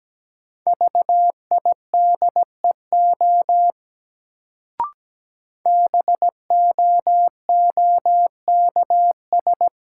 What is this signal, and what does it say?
Morse code, heard as a keyed tone, about 17 wpm, 700 Hz. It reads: VIDEO BOOKS
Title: Sentences from Top 1000 Words - Review Part 4 - New Word Drill - 17wpm